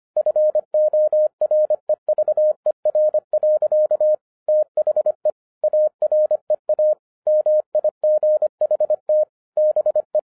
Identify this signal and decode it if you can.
Morse code, heard as a keyed tone, about 25 wpm, 610 Hz. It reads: FOREVER. THE AREA MIGHT BE
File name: 2011-07-mp3 / n39 THE 3.11 2 25wpm